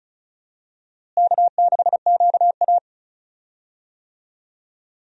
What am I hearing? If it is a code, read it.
Morse code, heard as a keyed tone, about 36 wpm, 700 Hz. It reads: K6QA
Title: USA Call Signs - 4 Characters - Repeated 2x - 35wpm